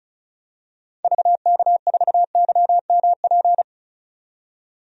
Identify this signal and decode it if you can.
Morse code, heard as a keyed tone, about 34 wpm, 700 Hz. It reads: VK4YMP
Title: DXCC-100 Call Signs - 6 Characters - 35wpm